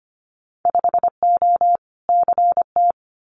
Morse code, heard as a keyed tone, about 25 wpm, 700 Hz. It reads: 5O CT
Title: Winter Field Day Exchanges Encore 25wpm